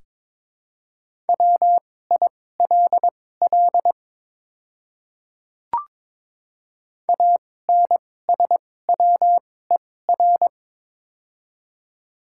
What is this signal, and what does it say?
Morse code, heard as a keyed tone, about 23 wpm, spaced as for 16 wpm, 700 Hz. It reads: WILL ANSWER
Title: 2x Character Spacing - Top 300 Words - 22wpm